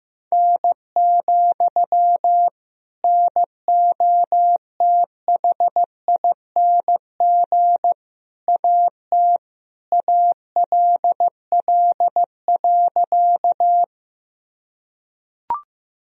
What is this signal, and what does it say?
Morse code, heard as a keyed tone, about 15 wpm, 700 Hz. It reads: N, NOTHING AT ALL.
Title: Sentences from Top 500 Words and Abbreviations 15wpm